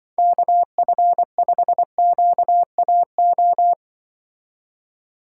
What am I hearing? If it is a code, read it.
Morse code, heard as a keyed tone, about 23 wpm, 700 Hz. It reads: KF5QAO
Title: Speed Racing - USA 2x3 Call Signs 20wpm